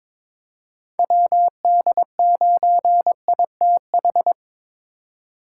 Morse code, heard as a keyed tone, about 22 wpm, 700 Hz. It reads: WD9ITH
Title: USA Call Signs - 6 Characters - 3x Character Spacing - 22wpm